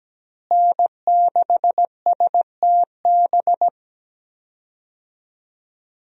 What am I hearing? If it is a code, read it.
Morse code, heard as a keyed tone, about 17 wpm, 700 Hz. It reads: N6STB
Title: USA Call Signs - 5 Characters - Repeated 2x - 17wpm